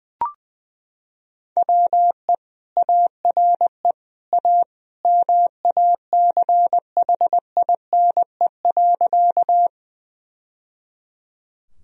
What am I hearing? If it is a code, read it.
Morse code, heard as a keyed tone, about 20 wpm, 700 Hz. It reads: WE ARE A MACHINE.
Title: Sentences from Top 1000 Words - Review Part 2 - 20wpm